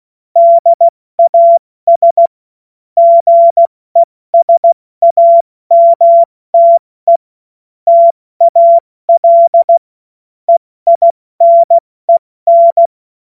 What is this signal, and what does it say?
Morse code, heard as a keyed tone, about 16 wpm, 680 Hz. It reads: DAS GESAMTE TAL EINEN
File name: Morsekurs / Witze / Witz 26 Dezember 2023 70 F80